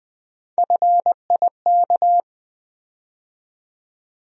Morse code, heard as a keyed tone, about 20 wpm, 700 Hz. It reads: FIK
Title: USA Call Sign Postfixes - 3 Letters 20wpm